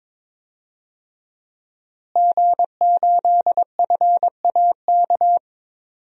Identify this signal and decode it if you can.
Morse code, heard as a keyed tone, about 22 wpm, 700 Hz. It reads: G8FAK
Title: DXCC-100 Call Signs - 5 Characters - Repeated 2x - 22wpm